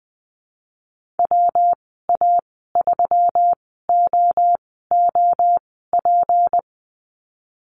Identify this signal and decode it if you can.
Morse code, heard as a keyed tone, about 20 wpm, spaced as for 15 wpm, 700 Hz. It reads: WA3OOP
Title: 167 Lesson 27g - Number 7 - Call Signs 20wpm